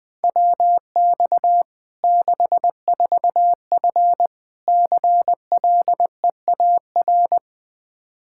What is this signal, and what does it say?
Morse code, heard as a keyed tone, about 20 wpm, 700 Hz. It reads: WX 64F CLEAR
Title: Practice QSO - Exchange 2 - Info - 20wpm